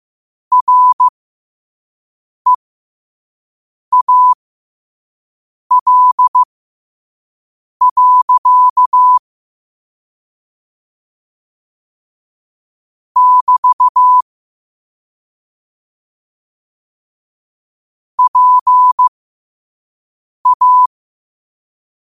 Morse code, heard as a keyed tone, about 15 wpm, spaced as for 5 wpm, 1000 Hz. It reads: REAL. <BT> PA